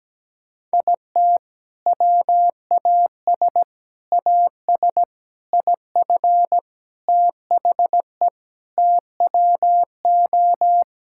Morse code, heard as a keyed tone, about 17 wpm, 700 Hz. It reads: IT WAS AS IF THE TWO
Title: Sentences from Top 1000 Words - Review Part 6 - 17wpm